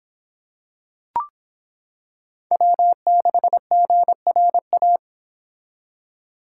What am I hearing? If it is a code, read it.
Morse code, heard as a keyed tone, about 25 wpm, 700 Hz. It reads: W6GRA